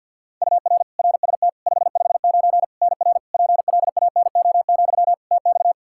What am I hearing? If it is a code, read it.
Morse code, heard as a keyed tone, about 46 wpm, 700 Hz. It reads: UR RST 559 NR PLANO, TX
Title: Practice QSO - Exchange 1 - Beginning - 50wpm